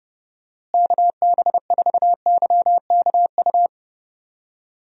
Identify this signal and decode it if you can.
Morse code, heard as a keyed tone, about 29 wpm, 700 Hz. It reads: KB4YKU
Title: USA Call Signs - 6 Characters - Repeated 2x - 30wpm